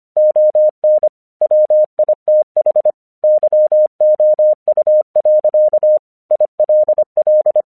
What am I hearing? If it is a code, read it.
Morse code, heard as a keyed tone, about 25 wpm, 610 Hz. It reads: ON WITH YOU. ILL
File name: 1 English 25wpm / AQC22